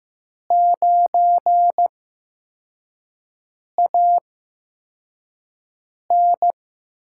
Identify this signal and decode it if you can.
Morse code, heard as a keyed tone, about 15 wpm, spaced as for 4 wpm, 700 Hz. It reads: 9AN
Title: USA Call Signs - 5 Characters - 8x Character Spacing - 15wpm